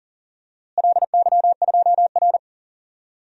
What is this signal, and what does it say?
Morse code, heard as a keyed tone, about 40 wpm, 700 Hz. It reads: LY2R
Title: DXCC-100 Call Signs - 4 Characters - Repeated 2x - 40wpm